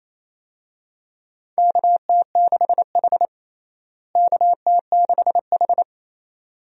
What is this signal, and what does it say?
Morse code, heard as a keyed tone, about 28 wpm, 700 Hz. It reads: KT6H KT6H
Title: USA Call Signs - 4 Characters - Repeated 2x - 28wpm